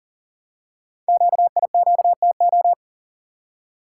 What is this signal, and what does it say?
Morse code, heard as a keyed tone, about 44 wpm, 700 Hz. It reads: QIQTO
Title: Code Groups - 5 Letters - 40wpm